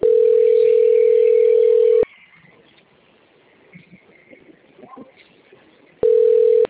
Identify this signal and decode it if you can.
Ringback tone (North America), 440 + 480 Hz.